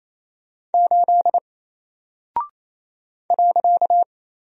Morse code, heard as a keyed tone, about 28 wpm, 700 Hz. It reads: 8 .
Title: Single Character (Letter:Number:Punctuation:Prosign) 28wpm